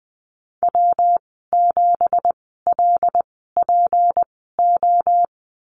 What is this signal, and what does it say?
Morse code, heard as a keyed tone, about 20 wpm, spaced as for 15 wpm, 700 Hz. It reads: W7LPO
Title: 186 Lesson 30g - Letter b - Call Signs 20wpm